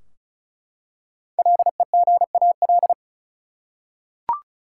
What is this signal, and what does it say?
Morse code, heard as a keyed tone, about 36 wpm, 700 Hz. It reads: LEGAL
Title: Five Letter Words - 35wpm